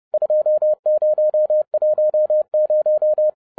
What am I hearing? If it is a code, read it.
Morse code, heard as a keyed tone, about 30 wpm, 610 Hz. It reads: 2010
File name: 2012-01-mp3 / n11-I DIDNT MISS THE SOLAR CYCLE 24-30wpm